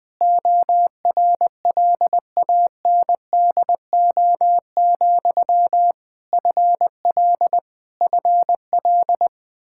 Morse code, heard as a keyed tone, about 19 wpm, 700 Hz. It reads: ORLANDO, FL FL
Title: Practice QSO - Exchange 1 - Beginning - 20wpm